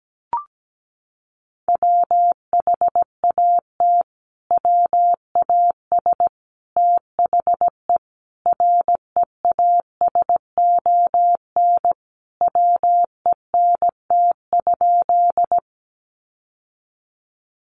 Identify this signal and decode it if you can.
Morse code, heard as a keyed tone, about 17 wpm, 700 Hz. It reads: WHAT WAS THE REASON WENT?
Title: Sentences from Top 700 Words 17wpm